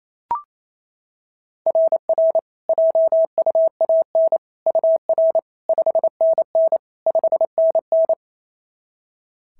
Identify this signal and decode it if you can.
Morse code, heard as a keyed tone, about 25 wpm, 650 Hz. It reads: RR JUAN UR 5NN 5NN
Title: SOTA QSO Practice 28wpm